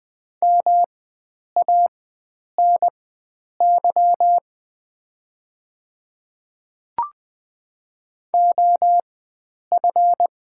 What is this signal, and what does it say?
Morse code, heard as a keyed tone, about 20 wpm, spaced as for 9 wpm, 700 Hz. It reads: MANY OF
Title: 4x Character Spacing - Top 100 Words - 20wpm